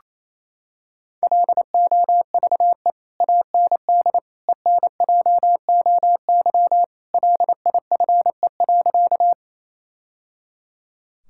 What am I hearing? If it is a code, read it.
Morse code, heard as a keyed tone, about 28 wpm, 700 Hz. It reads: LOVE AND ENJOY LIFE.
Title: Sentences from Top 1000 Words - Review Part 5 - 28wpm